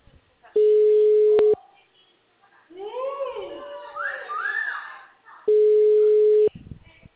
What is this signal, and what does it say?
Ringback tone (Europe, 425 Hz), with speech.